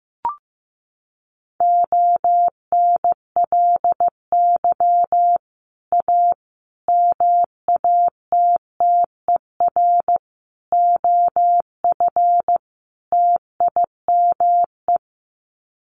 Morse code, heard as a keyed tone, about 15 wpm, 700 Hz. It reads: ONLY A MATTER OF TIME
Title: Sets of 5 Words Encore 15wpm